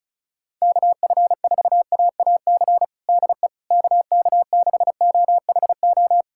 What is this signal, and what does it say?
Morse code, heard as a keyed tone, about 36 wpm, 700 Hz. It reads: KF4AAC DE KK6OHO
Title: USA Callsign DE Callsign 35wpm